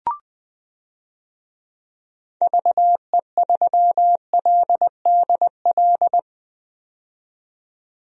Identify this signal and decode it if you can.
Morse code, heard as a keyed tone, about 21 wpm, 700 Hz. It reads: VE3LDL
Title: DXCC-100 Call Signs - 6 Characters - 20wpm